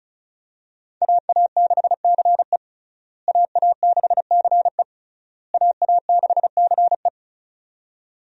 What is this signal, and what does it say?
Morse code, heard as a keyed tone, about 34 wpm, 700 Hz. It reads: AA6CE AA6CE AA6CE
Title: USA Call Signs - 5 Characters - Repeated 3x - 35wpm